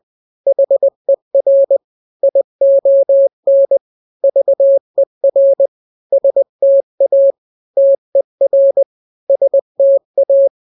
Morse code, heard as a keyed tone, about 20 wpm, 550 Hz. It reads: HER ION VER STA TER STA
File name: n-grams / 3-grams 20 18 WPM